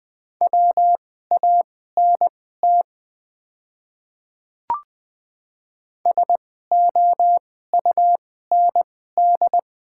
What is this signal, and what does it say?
Morse code, heard as a keyed tone, about 20 wpm, spaced as for 14 wpm, 700 Hz. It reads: WANT SOUND
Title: 2x Character Spacing - Top 300 Words - 20wpm